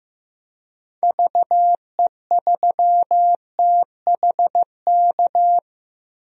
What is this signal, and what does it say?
Morse code, heard as a keyed tone, about 15 wpm, 700 Hz. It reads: VE3THK
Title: DXCC-100 Call Signs - 6 Characters - Repeated 3x - 15wpm